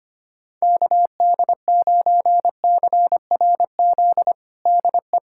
Morse code, heard as a keyed tone, about 25 wpm, 700 Hz. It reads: KD9CRZ DE
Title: USA Callsign DE Callsign 25wpm